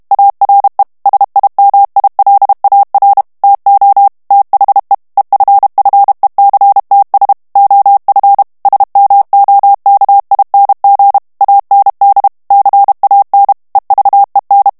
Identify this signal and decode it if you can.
Morse code, heard as a keyed tone, about 32 wpm, 800 Hz. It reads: ARE SIMILAR TO THE EFFECTS OF SMOKING AND CAN EVEN
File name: morse mp3 / ANGLAIS texte clair 1 vitesse 30 mots minute